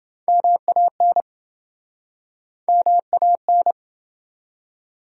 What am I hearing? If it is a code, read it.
Morse code, heard as a keyed tone, about 31 wpm, 700 Hz. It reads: MAN MAN